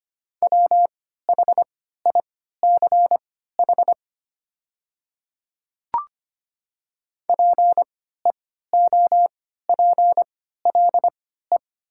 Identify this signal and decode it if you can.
Morse code, heard as a keyed tone, about 25 wpm, spaced as for 14 wpm, 700 Hz. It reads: WHICH PEOPLE
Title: 3x Character Spacing - Top 200 Words - 25wpm